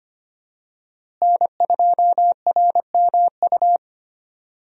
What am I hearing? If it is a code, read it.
Morse code, heard as a keyed tone, about 25 wpm, 700 Hz. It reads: N2RMU
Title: USA Call Signs - 5 Characters - Repeated 2x - 25wpm